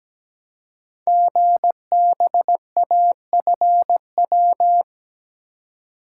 Morse code, heard as a keyed tone, about 17 wpm, 700 Hz. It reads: GBAFW
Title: Code Groups - 5 Letters - 17wpm